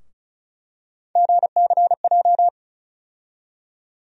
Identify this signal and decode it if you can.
Morse code, heard as a keyed tone, about 36 wpm, 700 Hz. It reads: GCJ
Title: USA Call Sign Postfixes - 3 Letters 35wpm